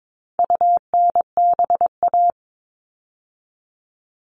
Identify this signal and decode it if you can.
Morse code, heard as a keyed tone, about 22 wpm, 700 Hz. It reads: UNBA
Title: All Cut Numbers - 4-Digit 22wpm